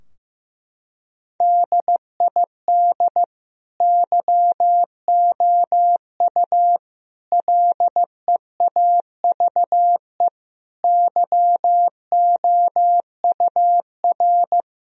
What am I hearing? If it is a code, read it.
Morse code, heard as a keyed tone, about 15 wpm, 700 Hz. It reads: DID YOU LEAVE YOUR
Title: Sentences from Top 1000 Words - Review Part 3 - 15wpm